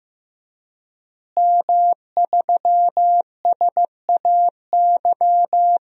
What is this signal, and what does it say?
Morse code, heard as a keyed tone, about 15 wpm, 700 Hz. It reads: M3SAY
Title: DXCC-100 Call Signs - 5 Characters - Repeated 3x - 15wpm